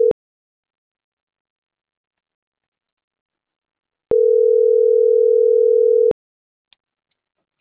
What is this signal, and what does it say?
Ringback tone (North America), 440 + 480 Hz.